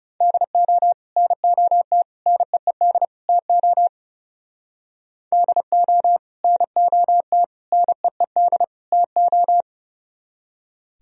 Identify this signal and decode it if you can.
Morse code, heard as a keyed tone, about 36 wpm, 700 Hz. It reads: DO NOT NEED TO DO NOT NEED TO
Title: Speed-Racing - Sets of 4 Words 30wpm